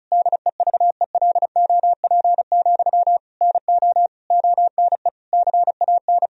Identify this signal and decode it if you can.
Morse code, heard as a keyed tone, about 36 wpm, 700 Hz. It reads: DEVELOP, NO ONE CAN
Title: Sentences from Top 1000 Words - Review Part 2 - 35wpm